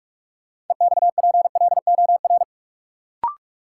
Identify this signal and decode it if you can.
Morse code, heard as a keyed tone, about 46 wpm, 700 Hz. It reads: EXPLOR
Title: Sentences from Top 1100 Words - New Word Drill - 45wpm